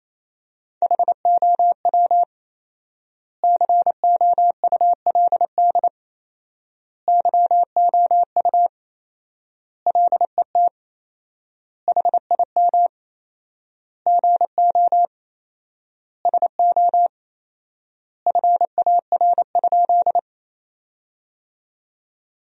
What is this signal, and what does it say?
Morse code, heard as a keyed tone, about 28 wpm, 700 Hz. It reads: HOW COULD YOU LET HIM GO SO FAR?
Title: 4x Word-Spacing - Sentences from Top 300 Words - 28wpm